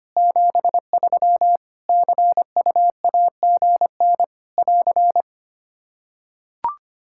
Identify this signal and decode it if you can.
Morse code, heard as a keyed tone, about 25 wpm, 700 Hz. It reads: 73 CUAGN <AR>
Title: K1USN SST QSO Practice 25wpm